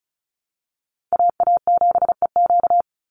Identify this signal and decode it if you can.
Morse code, heard as a keyed tone, about 34 wpm, 700 Hz. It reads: AA7EQ
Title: NA SOTA Chaser Callsigns 35wpm